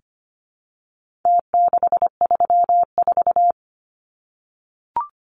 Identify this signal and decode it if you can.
Morse code, heard as a keyed tone, about 25 wpm, 700 Hz. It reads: T634
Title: Numbers - Contest Style - 4-Digit 25wpm